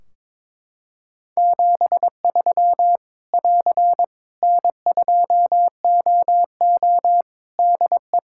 Morse code, heard as a keyed tone, about 22 wpm, 700 Hz. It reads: 73 <AR> N2OO DE
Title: Practice QSO - Exchange 4 - Ending - 22wpm